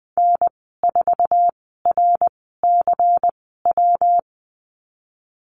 Morse code, heard as a keyed tone, about 20 wpm, spaced as for 15 wpm, 700 Hz. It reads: N4RCW
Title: 167 Lesson 27g - Number 7 - Call Signs 20wpm